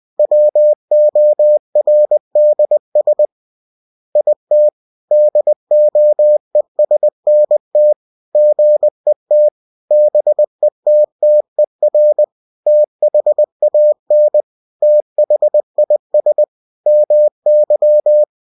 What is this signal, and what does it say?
Morse code, heard as a keyed tone, about 20 wpm, 600 Hz. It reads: WORDS IT DOESNT GET BETTER THAN THIS MY